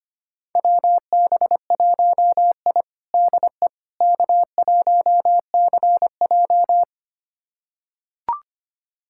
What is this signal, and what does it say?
Morse code, heard as a keyed tone, about 25 wpm, 700 Hz. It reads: WB1I DE K1CJ
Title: USA Callsign DE Callsign - 4 Characters - 25wpm